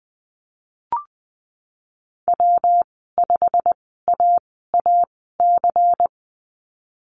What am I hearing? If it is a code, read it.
Morse code, heard as a keyed tone, about 20 wpm, spaced as for 15 wpm, 700 Hz. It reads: W5AAC
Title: 154 Lesson 25g - Letter p - Call Signs 20wpm